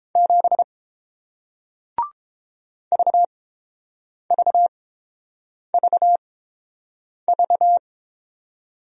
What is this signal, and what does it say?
Morse code, heard as a keyed tone, about 30 wpm, spaced as for 7 wpm, 700 Hz. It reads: 7 VVVV